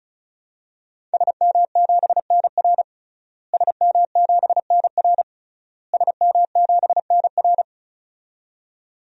Morse code, heard as a keyed tone, about 36 wpm, 700 Hz. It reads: SM7NR SM7NR SM7NR
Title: DXCC-100 Call Signs - 5 Characters - Repeated 3x - 35wpm